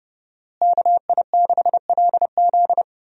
Morse code, heard as a keyed tone, about 30 wpm, 700 Hz. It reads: KI6LZ
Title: USA Call Signs - 5 Characters - Repeated 3x - 30wpm